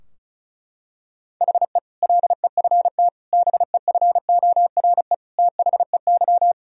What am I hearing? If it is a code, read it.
Morse code, heard as a keyed tone, about 36 wpm, 700 Hz. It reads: HE LEFT BEFORE THEY
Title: Sentences from Top 1000 Words - Review Part 4 - 35wpm